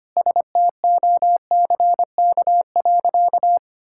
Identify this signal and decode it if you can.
Morse code, heard as a keyed tone, about 25 wpm, 700 Hz. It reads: STOCK.
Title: Sentences from Top 1000 Words - Review Part 5 - 25wpm